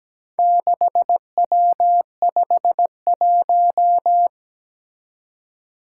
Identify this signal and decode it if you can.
Morse code, heard as a keyed tone, about 17 wpm, 700 Hz. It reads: 6W51
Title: Four Challenging Letter-Number Combinations 17wpm